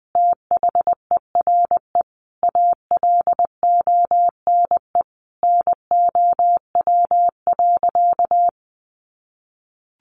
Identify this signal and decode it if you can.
Morse code, heard as a keyed tone, about 19 wpm, 700 Hz. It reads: THERE ALONE NOW.
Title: Sentences from Top 1100 Words 20wpm